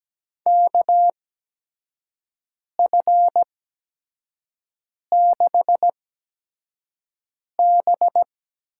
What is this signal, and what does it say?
Morse code, heard as a keyed tone, about 17 wpm, spaced as for 5 wpm, 700 Hz. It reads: KF6B
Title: USA Call Signs - 6 Characters - 8x Character Spacing - 17wpm